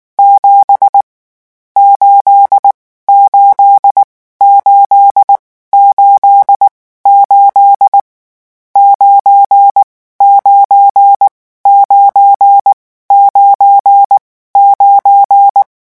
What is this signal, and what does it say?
Morse code, heard as a keyed tone, about 19 wpm, spaced as for 14 wpm, 800 Hz. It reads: 7 8 8 888 99 99 9
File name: morse mp3 / lecon12-vitesse 18